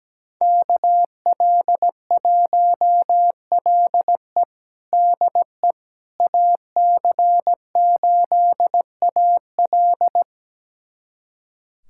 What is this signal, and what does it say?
Morse code, heard as a keyed tone, about 17 wpm, 700 Hz. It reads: KL1LE DE AC8AL
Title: USA Callsign DE Callsign - 5 Characters - 17wpm